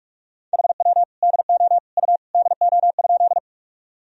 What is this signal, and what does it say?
Morse code, heard as a keyed tone, about 41 wpm, 700 Hz. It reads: HW DO U DO?
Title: Sentences from Top 100 Words and Abbreviations 45wpm